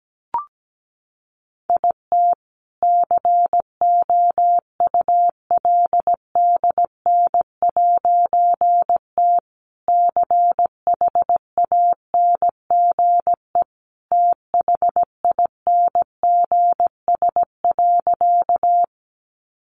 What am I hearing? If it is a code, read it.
Morse code, heard as a keyed tone, about 17 wpm, 700 Hz. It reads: IT COULDN'T CHANGE THINGS.
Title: A Sound Of Thunder 17wpm